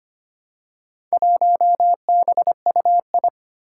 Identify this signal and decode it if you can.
Morse code, heard as a keyed tone, about 25 wpm, 700 Hz. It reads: 1BUI
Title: Four Challenging Letter-Number Combinations 25wpm